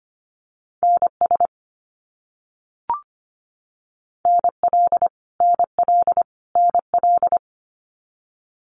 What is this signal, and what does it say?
Morse code, heard as a keyed tone, about 25 wpm, 700 Hz. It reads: NS NL NL NL
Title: Canadian Provinces and Territories - Repeated 3x - 25wpm